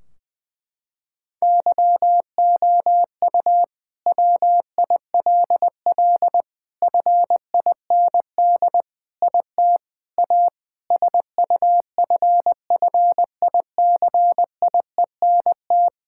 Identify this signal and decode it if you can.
Morse code, heard as a keyed tone, about 21 wpm, 700 Hz. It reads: YOU WILL FIND IT A SUFFICIENT
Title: Sentences from Top 1600 Words 20wpm